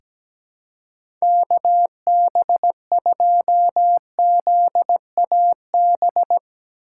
Morse code, heard as a keyed tone, about 17 wpm, 700 Hz. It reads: KB2ZAB
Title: USA Call Signs - 6 Characters - 3x Character Spacing - 17wpm